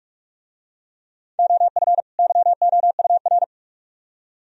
Morse code, heard as a keyed tone, about 43 wpm, 700 Hz. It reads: OF YOUR